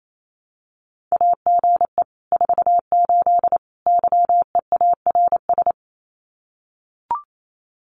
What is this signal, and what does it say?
Morse code, heard as a keyed tone, about 28 wpm, 700 Hz. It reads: AGE 48 YEARS
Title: Common QSO Phrases Encore 28wpm